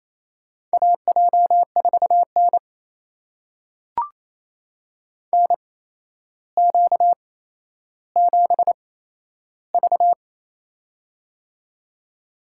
Morse code, heard as a keyed tone, about 27 wpm, spaced as for 12 wpm, 700 Hz. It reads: AJ4N NQ7V
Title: USA Call Signs - 4 Characters - 8x Character Spacing - 28wpm